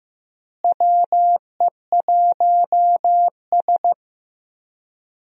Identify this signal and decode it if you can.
Morse code, heard as a keyed tone, about 15 wpm, 700 Hz. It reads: WE1S
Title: USA Call Signs - 4 Characters - Repeated 2x - 15wpm